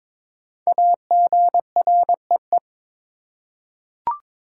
Morse code, heard as a keyed tone, about 22 wpm, 700 Hz. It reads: AGREE
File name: 5 Letter Words - First 3 Letters Given - 22wpm